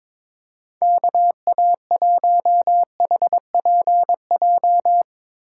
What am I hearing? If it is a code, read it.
Morse code, heard as a keyed tone, about 22 wpm, 700 Hz. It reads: KA1HPJ
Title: USA Call Signs - 6 Characters - 4x Character Spacing - 22wpm